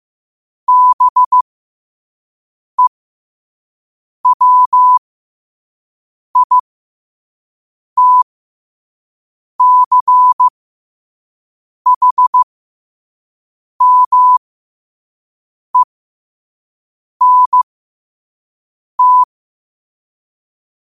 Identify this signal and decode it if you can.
Morse code, heard as a keyed tone, about 15 wpm, spaced as for 5 wpm, 1000 Hz. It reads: BEWITCHMENT
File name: 5wpm